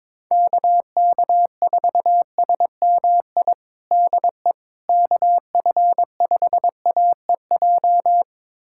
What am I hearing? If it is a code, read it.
Morse code, heard as a keyed tone, about 22 wpm, 700 Hz. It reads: KK4SMI DE KF5AEJ
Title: USA Callsign DE Callsign - 6 Characters - 22wpm